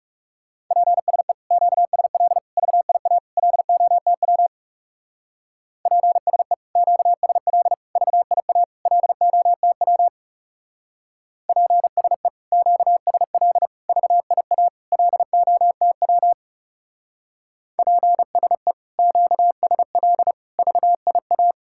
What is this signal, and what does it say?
Morse code, heard as a keyed tone, about 37 wpm, 700 Hz. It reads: PSE QSL VIA LOTW PSE QSL VIA LOTW PSE QSL VIA LOTW PSE QSL VIA
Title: Speed-Racing Common QSO Phrases Encore - 30wpm